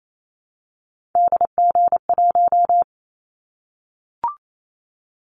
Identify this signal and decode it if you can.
Morse code, heard as a keyed tone, about 28 wpm, 700 Hz. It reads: DG1
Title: Callsign Prefixes for Top 25 DXCC - 28wpm